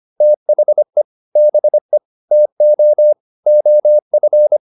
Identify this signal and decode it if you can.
Morse code, heard as a keyed tone, about 25 wpm, 600 Hz. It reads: THE BE TO OF
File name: Top-100-Words-25